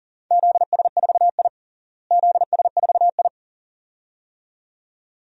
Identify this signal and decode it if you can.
Morse code, heard as a keyed tone, about 40 wpm, 700 Hz. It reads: ZS4I ZS4I
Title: DXCC-100 Call Signs - 4 Characters - Repeated 2x - 40wpm